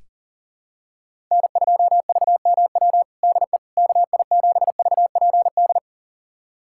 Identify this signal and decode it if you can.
Morse code, heard as a keyed tone, about 40 wpm, 700 Hz. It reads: N2VMW DE KI7VPD
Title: USA Callsign DE Callsign 40wpm